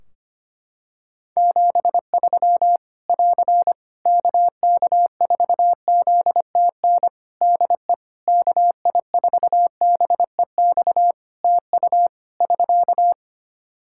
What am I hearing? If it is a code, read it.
Morse code, heard as a keyed tone, about 25 wpm, 700 Hz. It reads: 73 <AR> KK4ZTN DE KI4BEX TU <SK>
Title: K1USN SST QSO Practice 25wpm